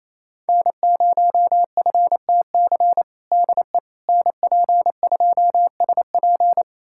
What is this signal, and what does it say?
Morse code, heard as a keyed tone, about 28 wpm, 700 Hz. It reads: N0FTC DE NP2SP
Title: USA Callsign DE Callsign - 5 Characters - 28wpm